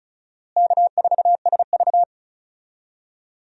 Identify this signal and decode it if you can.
Morse code, heard as a keyed tone, about 34 wpm, 700 Hz. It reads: K4SV
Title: USA Call Signs - 4 Characters - Repeated 2x - 35wpm